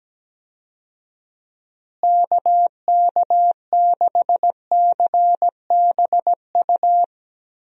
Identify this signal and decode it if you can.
Morse code, heard as a keyed tone, about 17 wpm, 700 Hz. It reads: KK6CBU
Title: USA Call Signs - 6 Characters - Repeated 3x - 17wpm